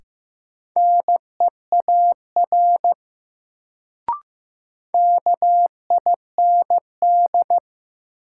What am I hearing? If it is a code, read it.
Morse code, heard as a keyed tone, about 15 wpm, 700 Hz. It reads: NEAR KIND
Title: Sentences from Top 200 Words - New Word Drill - 15wpm